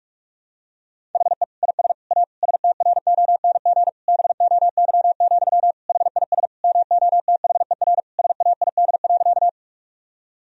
Morse code, heard as a keyed tone, about 46 wpm, 700 Hz. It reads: HE IS A STRONG BOY, HIS MOTHER SAID.
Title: Sentences from Top 1000 Words - Review Part 3 - 45wpm